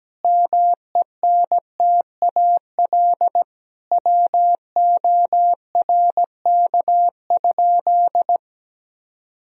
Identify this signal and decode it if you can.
Morse code, heard as a keyed tone, about 17 wpm, 700 Hz. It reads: MENTAL WORK?
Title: Sentences from Top 1000 Words - Review Part 4 - 17wpm